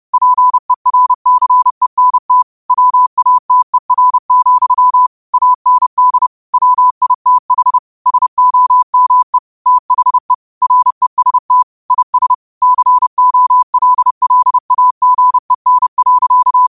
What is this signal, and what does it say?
Morse code, heard as a keyed tone, about 30 wpm, 1000 Hz. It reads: PERCENT WATER, AND WITH SOME THE REST IS COLLAGEN.
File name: qotdfiles / 30wpm